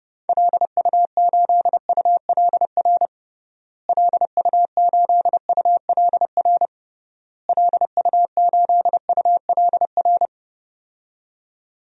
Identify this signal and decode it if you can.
Morse code, heard as a keyed tone, about 30 wpm, 700 Hz. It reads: LU8ULR LU8ULR LU8ULR
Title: DXCC-100 Call Signs - 6 Characters - Repeated 3x - 30wpm